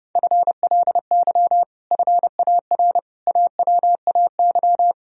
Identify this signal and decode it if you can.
Morse code, heard as a keyed tone, about 30 wpm, 700 Hz. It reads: FLY FAR AWAY
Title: Sentences from Top 1000 Words - Review Part 2 - 30wpm